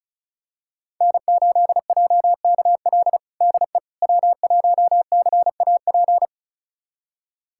Morse code, heard as a keyed tone, about 36 wpm, 700 Hz. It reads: N8JKL DE W1CAP
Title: USA Callsign DE Callsign - 5 Characters - 35wpm